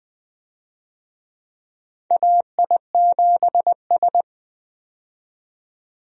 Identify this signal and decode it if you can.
Morse code, heard as a keyed tone, about 20 wpm, 700 Hz. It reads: AI7S
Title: USA Call Signs - 4 Characters - 20wpm